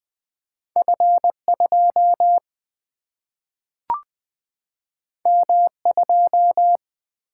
Morse code, heard as a keyed tone, about 20 wpm, 700 Hz. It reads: F2 M2
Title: Single-Letter Single-Number 20wpm